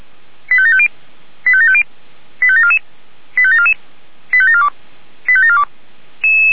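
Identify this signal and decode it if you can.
Five-tone selective call (ZVEI), heard as quick runs of five tones, tones of about 70 ms. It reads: 85750 85750 85744 85744 85742 85742
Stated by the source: ZVEI five-tone selective call 85749, received by SDR and archived by the Feuerwehr Orlamünde fire brigade (f85360).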